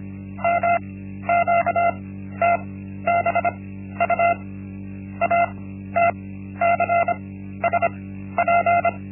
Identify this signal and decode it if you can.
Morse code, heard as a keyed tone, about 25 wpm, spaced as for 13 wpm, 690 Hz. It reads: MQTBU ATCSP
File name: Linksmakalnis-1